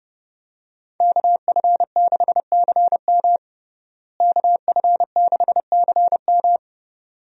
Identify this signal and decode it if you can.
Morse code, heard as a keyed tone, about 32 wpm, 700 Hz. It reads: KF6CM KF6CM
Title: USA Call Signs - 5 Characters - Repeated 2x - 30wpm